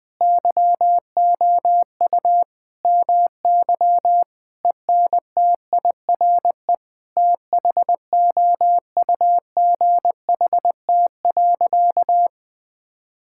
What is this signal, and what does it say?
Morse code, heard as a keyed tone, about 20 wpm, 700 Hz. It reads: YOU MY ENTIRE THOUGHT.
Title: Sentences from Top 1000 Words - Review Part 5 - 20wpm